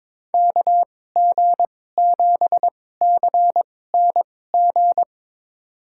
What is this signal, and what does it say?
Morse code, heard as a keyed tone, about 22 wpm, spaced as for 16 wpm, 700 Hz. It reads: KG7CNG
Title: USA Call Signs - 6 Characters - 2x Character Spacing - 22wpm